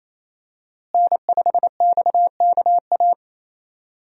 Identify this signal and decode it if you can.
Morse code, heard as a keyed tone, about 28 wpm, 700 Hz. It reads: N5XKA
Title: USA Call Signs - 5 Characters - Repeated 2x - 28wpm